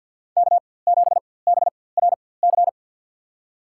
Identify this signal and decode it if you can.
Morse code, heard as a keyed tone, about 53 wpm, spaced as for 24 wpm, 700 Hz. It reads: K7BRC
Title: USA Call Signs - 5 Characters - 4x Character Spacing - 50wpm